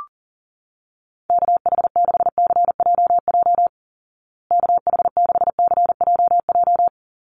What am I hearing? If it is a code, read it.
Morse code, heard as a keyed tone, about 40 wpm, 700 Hz. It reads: KH6CJJ KH6CJJ
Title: Call Signs Repeated 2x - 40wpm